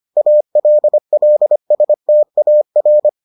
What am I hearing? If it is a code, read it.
Morse code, heard as a keyed tone, about 25 wpm, 600 Hz. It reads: ALLSTAR